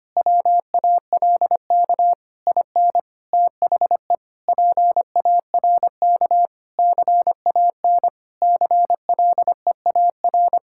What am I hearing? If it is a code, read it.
Morse code, heard as a keyed tone, about 25 wpm, 700 Hz. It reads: WALK IN THE PARK CAN CLEAR
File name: practice / 25wpm / Difficult Sentences with Hints - 25wpm